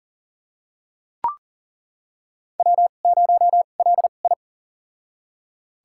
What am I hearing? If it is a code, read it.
Morse code, heard as a keyed tone, about 40 wpm, spaced as for 29 wpm, 700 Hz. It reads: W0LI
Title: USA Call Signs - 4 Characters - 2x Character Spacing - 40wpm